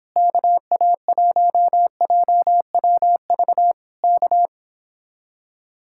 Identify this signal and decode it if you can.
Morse code, heard as a keyed tone, about 26 wpm, 700 Hz. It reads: KA1JWV K